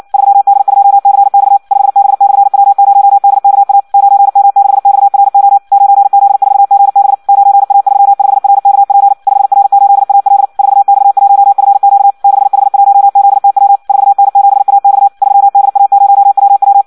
Morse code, heard as a keyed tone, about 56 wpm, 800 Hz. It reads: 3D2CX 5B/W0DKA 9M6XRO 9Z4GB 9A35RKP 5R8AL 4X1FQ 6H1ZTW 4N7AX 3DA0WW
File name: Audio practica CW / qrq-tx-58 2